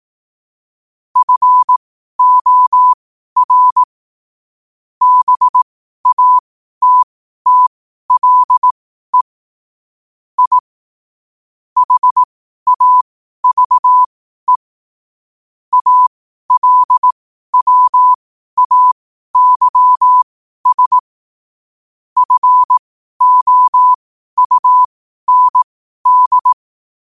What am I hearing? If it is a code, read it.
Morse code, heard as a keyed tone, about 18 wpm, spaced as for 12 wpm, 1000 Hz. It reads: FOR BATTLE I HAVE ALWAYS FOUND
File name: qotdfiles / 12wpm